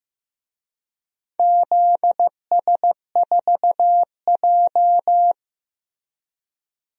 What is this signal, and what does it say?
Morse code, heard as a keyed tone, about 15 wpm, 700 Hz. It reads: ZS4J
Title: DXCC-100 Call Signs - 4 Characters - Repeated 3x - 15wpm